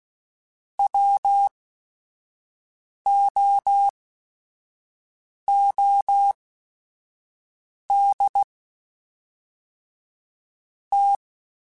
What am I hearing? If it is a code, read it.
Morse code, heard as a keyed tone, about 16 wpm, spaced as for 5 wpm, 800 Hz. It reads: WOOD T
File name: QSOs / QSOs-mp3s / WD0BGP 1@5